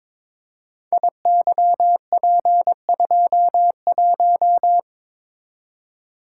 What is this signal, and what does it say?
Morse code, heard as a keyed tone, about 22 wpm, 700 Hz. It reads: IYP21
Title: Code Groups - 5 Letter:Numbers - 22wpm